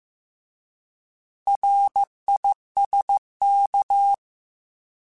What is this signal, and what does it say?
Morse code, heard as a keyed tone, about 15 wpm, 800 Hz. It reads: RISK